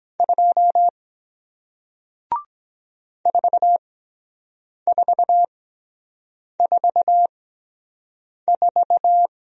Morse code, heard as a keyed tone, about 23 wpm, spaced as for 7 wpm, 700 Hz. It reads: 2 4444